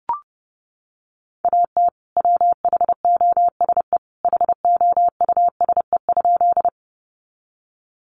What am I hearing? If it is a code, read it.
Morse code, heard as a keyed tone, about 30 wpm, 700 Hz. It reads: AT WHOSE HOUSE?
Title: Sentences from Top 1200 Words 30wpm